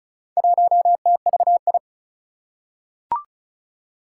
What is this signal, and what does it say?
Morse code, heard as a keyed tone, about 34 wpm, 700 Hz. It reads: 1TVI
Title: Four Challenging Letter-Number Combinations 35wpm